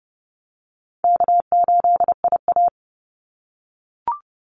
Morse code, heard as a keyed tone, about 29 wpm, 700 Hz. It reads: K8IA
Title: Easy Call Signs - 30wpm